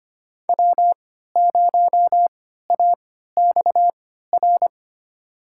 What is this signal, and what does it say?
Morse code, heard as a keyed tone, about 25 wpm, spaced as for 14 wpm, 700 Hz. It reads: W0AXR
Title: USA Call Signs - 5 Characters - 3x Character Spacing - 25wpm